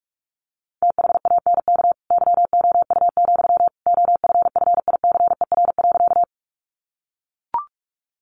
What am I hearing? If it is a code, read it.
Morse code, heard as a keyed tone, about 44 wpm, 700 Hz. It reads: THANK YOU, OFFICER.
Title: Sentences from Top 1200 Words 45wpm